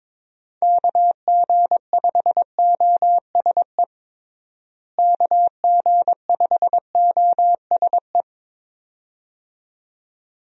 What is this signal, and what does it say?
Morse code, heard as a keyed tone, about 22 wpm, 700 Hz. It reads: KG5OSE KG5OSE
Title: USA Call Signs - 6 Characters - Repeated 2x - 22wpm